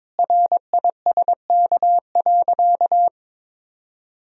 Morse code, heard as a keyed tone, about 22 wpm, 700 Hz. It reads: RISK.
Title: Sentences from Top 1000 Words - Review Part 4 - 22wpm